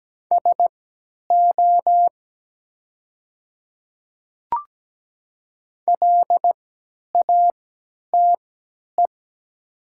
Morse code, heard as a keyed tone, about 17 wpm, spaced as for 10 wpm, 700 Hz. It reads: SO LATE
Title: 3x Character Spacing - Top 300 Words - 17wpm